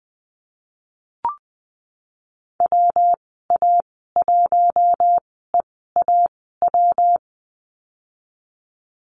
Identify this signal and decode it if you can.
Morse code, heard as a keyed tone, about 20 wpm, spaced as for 15 wpm, 700 Hz. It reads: WA1EAW
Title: 231 Lesson 37g - Letter x - Call Signs 20wpm